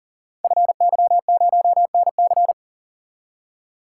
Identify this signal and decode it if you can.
Morse code, heard as a keyed tone, about 38 wpm, 690 Hz. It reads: FY0NC
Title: Code Groups - 5 Letter:Numbers - 40wpm